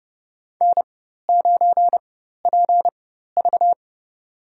Morse code, heard as a keyed tone, about 30 wpm, spaced as for 14 wpm, 700 Hz. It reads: N9PV
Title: USA Call Signs - 4x Character Spacing - 30wpm